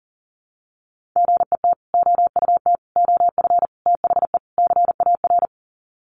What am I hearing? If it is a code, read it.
Morse code, heard as a keyed tone, about 40 wpm, 700 Hz. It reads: GET OUT OF THE CAR
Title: Sets of 5 Words Encore2 40wpm